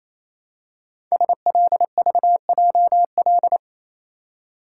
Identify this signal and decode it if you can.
Morse code, heard as a keyed tone, about 27 wpm, 700 Hz. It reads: SLVJL
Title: Code Groups - 5 Letters - 28wpm